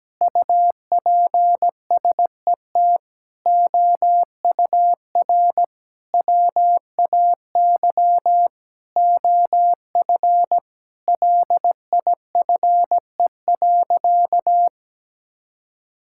Morse code, heard as a keyed tone, about 17 wpm, 700 Hz. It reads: UPSET OUR WAY OF LIFE.
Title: Sentences from Top 1000 Words - Review Part 6 - 17wpm